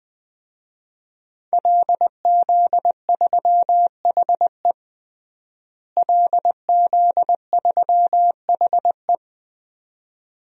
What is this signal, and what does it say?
Morse code, heard as a keyed tone, about 20 wpm, 700 Hz. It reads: LZ3HE LZ3HE
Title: DXCC-100 Call Signs - 5 Characters - Repeated 2x - 20wpm